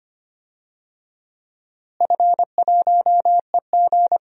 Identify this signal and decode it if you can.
Morse code, heard as a keyed tone, about 25 wpm, 700 Hz. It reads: F1EG
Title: DXCC-100 Call Signs - 4 Characters - Repeated 2x - 25wpm